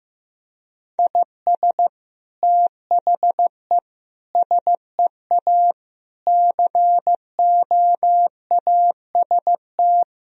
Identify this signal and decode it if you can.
Morse code, heard as a keyed tone, about 16 wpm, 700 Hz. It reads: IS THE SEA COAST
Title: Sentences from Top 1000 Words - Review Part 4 - 15wpm